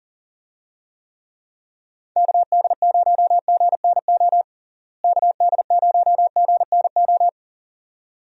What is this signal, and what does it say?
Morse code, heard as a keyed tone, about 40 wpm, 700 Hz. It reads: KD0GNO KD0GNO
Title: USA Call Signs - 6 Characters - Repeated 2x - 40wpm